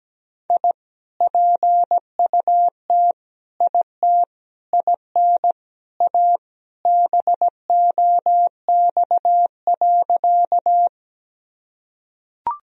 Morse code, heard as a keyed tone, about 17 wpm, 700 Hz. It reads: I PUT IT IN A BOX.
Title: Sentences from Top 1000 Words - Review Part 2 - 17wpm